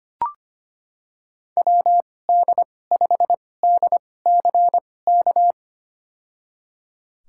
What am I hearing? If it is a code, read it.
Morse code, heard as a keyed tone, about 25 wpm, spaced as for 18 wpm, 700 Hz. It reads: WD5DCK
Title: USA Call Signs - 6 Characters - 2x Character Spacing - 25wpm